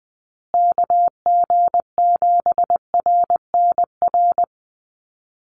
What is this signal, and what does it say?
Morse code, heard as a keyed tone, about 20 wpm, 700 Hz. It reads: KG7RNR
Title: USA Call Signs Repeated 2x 20wpm